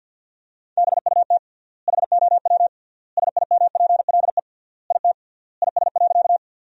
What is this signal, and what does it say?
Morse code, heard as a keyed tone, about 47 wpm, 700 Hz. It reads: BUT HOW SIMPLE IT IS.
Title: 3x Word-Spacing - Sentences from Top 400 Words - 50wpm